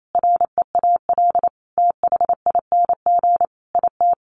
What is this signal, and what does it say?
Morse code, heard as a keyed tone, about 28 wpm, 700 Hz. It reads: REAL THING IT
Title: Sentences from Top 1300 Words 28wpm